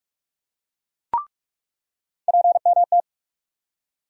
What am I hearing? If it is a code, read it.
Morse code, heard as a keyed tone, about 43 wpm, 700 Hz. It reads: PMT
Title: USA Call Sign Postfixes - 3 Letters 45wpm